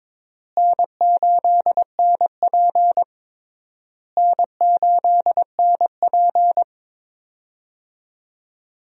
Morse code, heard as a keyed tone, about 22 wpm, 700 Hz. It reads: N8NP N8NP
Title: USA Call Signs - 4 Characters - Repeated 3x - 22wpm